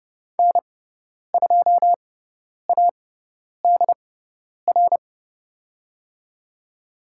Morse code, heard as a keyed tone, about 30 wpm, spaced as for 10 wpm, 700 Hz. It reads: N2ADR
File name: Lesson 16g - Letter c - Call Signs 30wpm